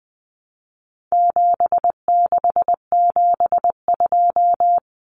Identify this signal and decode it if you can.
Morse code, heard as a keyed tone, about 20 wpm, 700 Hz. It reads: 7672
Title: Common Cut Numbers - 4-Digit 20wpm